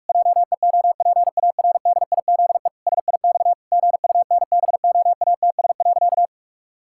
Morse code, heard as a keyed tone, about 46 wpm, 700 Hz. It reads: JEOPARDIZE SIX GUNBOATS.
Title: Pangram Challenge v2 - 45wpm